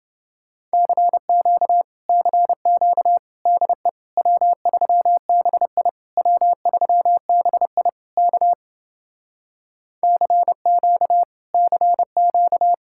Morse code, heard as a keyed tone, about 29 wpm, 700 Hz. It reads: CQ CQ DE W3BI W3BI K CQ CQ